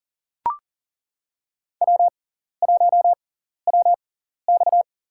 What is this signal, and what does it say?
Morse code, heard as a keyed tone, about 40 wpm, spaced as for 14 wpm, 700 Hz. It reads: W1WX
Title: USA Call Signs - 4 Characters - 6x Character Spacing - 40wpm